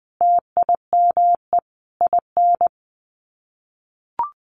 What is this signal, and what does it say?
Morse code, heard as a keyed tone, about 20 wpm, 700 Hz. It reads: TIME IN
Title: Sets of 5 Words 20wpm